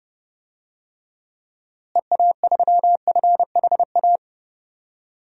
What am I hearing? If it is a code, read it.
Morse code, heard as a keyed tone, about 32 wpm, 700 Hz. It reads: EA3FHA
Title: DXCC-100 Call Signs - 6 Characters - Repeated 3x - 30wpm